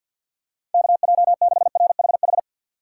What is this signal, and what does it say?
Morse code, heard as a keyed tone, about 53 wpm, 700 Hz. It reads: KJ6RHH
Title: USA Call Signs - 6 Characters - 8x Character Spacing - 50wpm